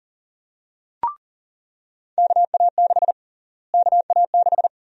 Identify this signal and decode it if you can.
Morse code, heard as a keyed tone, about 40 wpm, 700 Hz. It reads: KA6 KA6
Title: USA Call Sign Prefixes - Repeated 2x - 40wpm